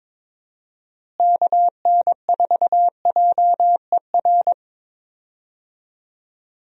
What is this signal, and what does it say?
Morse code, heard as a keyed tone, about 22 wpm, 700 Hz. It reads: KN4JER
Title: USA Call Signs - 6 Characters - Repeated 2x - 22wpm